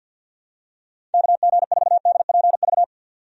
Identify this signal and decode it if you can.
Morse code, heard as a keyed tone, about 52 wpm, 700 Hz. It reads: KG4DPV
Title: USA Call Signs - 6 Characters - 50wpm